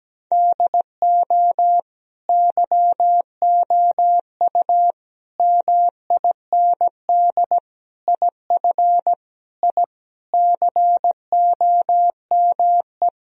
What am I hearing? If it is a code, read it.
Morse code, heard as a keyed tone, about 17 wpm, 700 Hz. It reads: DO YOU MIND IF I COME
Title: Sentences from Top 1000 Words - Review Part 2 - 17wpm